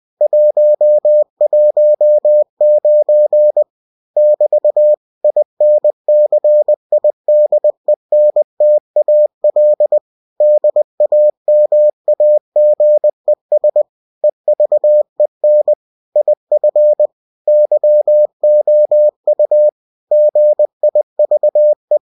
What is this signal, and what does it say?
Morse code, heard as a keyed tone, about 20 wpm, 600 Hz. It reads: 119 <BT> INCIDENTAL DAMAGES EVEN IF YOU GIVE
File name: cw-books / winnie-the-pooh / winnie-20wpm-page-0119